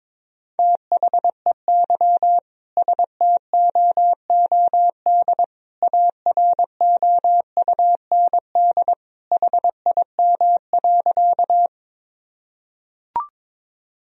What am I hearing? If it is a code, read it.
Morse code, heard as a keyed tone, about 23 wpm, 700 Hz. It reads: THEY STOOD AROUND HIM.
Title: Sentences from Top 1000 Words - Review Part 3 - 22wpm